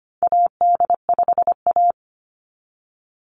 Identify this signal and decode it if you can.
Morse code, heard as a keyed tone, about 25 wpm, 690 Hz. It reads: AD5A
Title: NA SOTA Activator Callsigns 25wpm